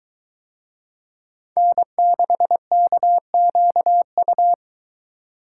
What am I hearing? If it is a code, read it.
Morse code, heard as a keyed tone, about 22 wpm, 700 Hz. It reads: N6KQU